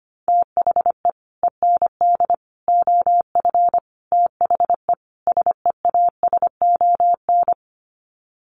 Morse code, heard as a keyed tone, about 25 wpm, 700 Hz. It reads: THE END OF THE SEASON
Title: Sets of 5 Words Encore 25wpm